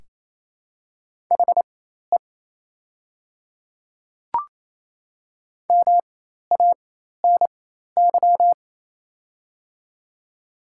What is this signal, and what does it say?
Morse code, heard as a keyed tone, about 28 wpm, spaced as for 13 wpm, 700 Hz. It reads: HE MANY
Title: 4x Character Spacing - Top 100 Words - 28wpm